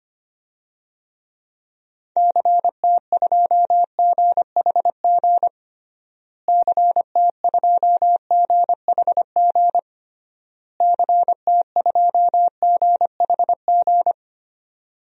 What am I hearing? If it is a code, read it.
Morse code, heard as a keyed tone, about 25 wpm, 700 Hz. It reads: CT2GHG CT2GHG CT2GHG
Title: DXCC-100 Call Signs - 6 Characters - Repeated 3x - 25wpm